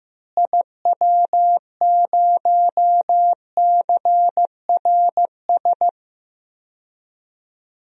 Morse code, heard as a keyed tone, about 15 wpm, 700 Hz. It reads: IW0CRS
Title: DXCC-100 Call Signs - 6 Characters - Repeated 3x - 15wpm